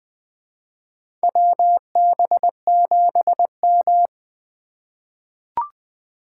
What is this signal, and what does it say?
Morse code, heard as a keyed tone, about 20 wpm, 700 Hz. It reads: WB7M
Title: USA Call Signs - 4 Characters - 3x Character Spacing - 20wpm